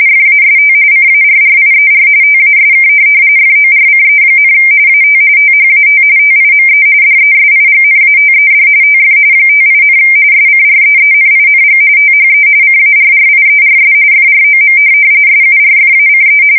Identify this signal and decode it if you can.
RTTY, about 45.45 baud. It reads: FAST CARRIER TASK FORCE, AT THAT TIME DESIGNATED TF 58, SCREENING THE CARRIERS FROM SURFACE
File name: fsk170 / USS-Massachusetts